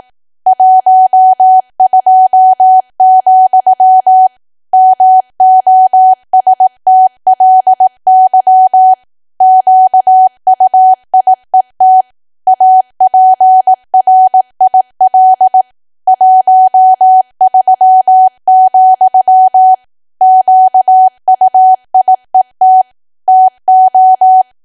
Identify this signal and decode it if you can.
Morse code, heard as a keyed tone, about 18 wpm, 750 Hz. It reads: 12, MOSTLY QUIET APRIL 13, QUIET TO
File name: cwa / W1AW-Level2 / 18WPM